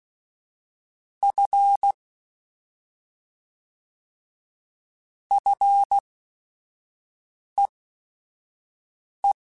Morse code, heard as a keyed tone, about 16 wpm, spaced as for 5 wpm, 800 Hz. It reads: F FEE
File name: Short Runs / Short Runs-mp3s / SinglePrefixes @5